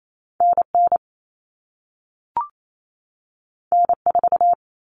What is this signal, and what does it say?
Morse code, heard as a keyed tone, about 28 wpm, 700 Hz. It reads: NN N4
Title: All Cut Numbers - 2-Digit 28wpm